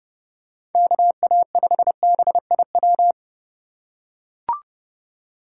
Morse code, heard as a keyed tone, about 32 wpm, 700 Hz. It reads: KA5BIW
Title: USA Call Signs - 6 Characters - 2x Character Spacing - 30wpm